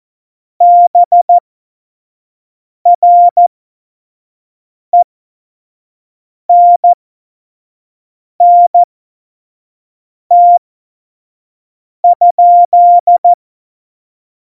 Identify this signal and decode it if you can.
Morse code, heard as a keyed tone, about 14 wpm, spaced as for 5 wpm, 700 Hz. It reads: BRENNT?
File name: Morsekurs / Witze / Witz 04 Juli 17, Tempo 25 F70